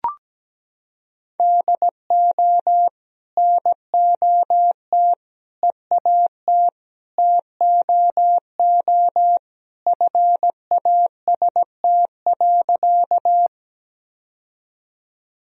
Morse code, heard as a keyed tone, about 17 wpm, 700 Hz. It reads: DO NOT EAT TOO FAST.
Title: Sentences from Top 1000 Words - Review Part 2 - 17wpm